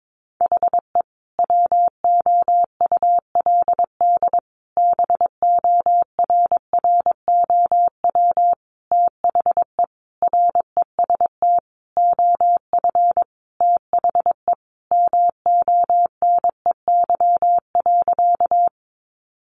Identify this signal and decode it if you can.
Morse code, heard as a keyed tone, about 22 wpm, 700 Hz. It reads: HE WOULD BORROW THE REST OF THE MONEY.
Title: Sentences from Top 1500 Words 22wpm